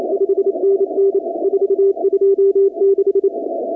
Morse code, heard as a keyed tone, about 29 wpm, 390 Hz. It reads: 5NN 426
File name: contests / 2019-cw-wpx-kt5j-k5pi- -2330 0-23